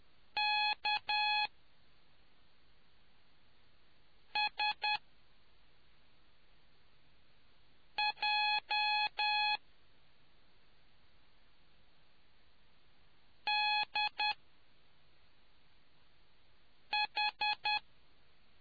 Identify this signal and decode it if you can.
Morse code, heard as a keyed tone, about 10 wpm, spaced as for 3 wpm, 840 Hz. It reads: KSJDH